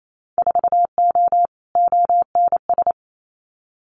Morse code, heard as a keyed tone, about 28 wpm, 700 Hz. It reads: 4O ONS
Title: Winter Field Day Exchanges 28wpm